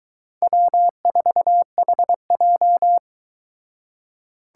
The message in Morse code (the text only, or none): W4HJ